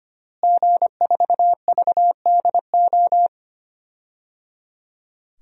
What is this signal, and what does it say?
Morse code, heard as a keyed tone, about 25 wpm, 700 Hz. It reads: G4VDO
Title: DXCC-100 Call Signs - 5 Characters - Repeated 3x - 25wpm